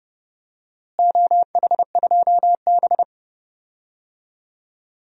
Morse code, heard as a keyed tone, about 32 wpm, 700 Hz. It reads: OH2B
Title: DXCC-100 Call Signs - 4 Characters - Repeated 3x - 30wpm